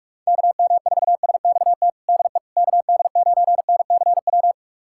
Morse code, heard as a keyed tone, about 41 wpm, 700 Hz. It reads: KM4SXT DE KD9NCW
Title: USA Callsign DE Callsign - 6 Characters - 45wpm